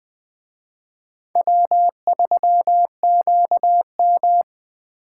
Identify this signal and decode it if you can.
Morse code, heard as a keyed tone, about 18 wpm, 700 Hz. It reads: W3QM
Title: Speed Racing - USA Call Signs 15wpm